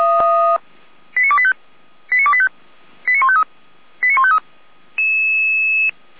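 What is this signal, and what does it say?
Five-tone selective call (ZVEI), heard as quick runs of five tones, tones of about 70 ms. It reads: 89285 89285 89153 89153
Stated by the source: ZVEI five-tone selective call 89020, received by SDR.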